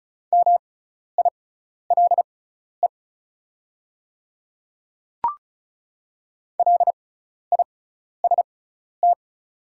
Morse code, heard as a keyed tone, about 36 wpm, spaced as for 12 wpm, 690 Hz. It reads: MILE LIST
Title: 6x Character Spacing - Top 500 Words - 35wpm